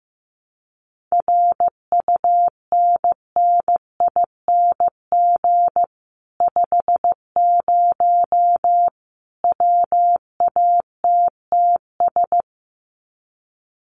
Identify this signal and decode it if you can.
Morse code, heard as a keyed tone, about 15 wpm, 700 Hz. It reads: RUNNING 50 WATTS
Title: Common QSO Phrases 15wpm